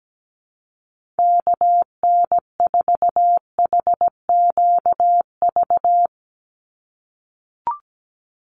Morse code, heard as a keyed tone, about 17 wpm, 700 Hz. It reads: KN4HQV
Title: USA Call Signs 17wpm